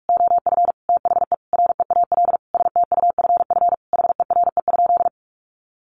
Morse code, heard as a keyed tone, about 44 wpm, 700 Hz. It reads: OF THE REAL STUFF HERE?
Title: Sentences from Top 1400 Words 45wpm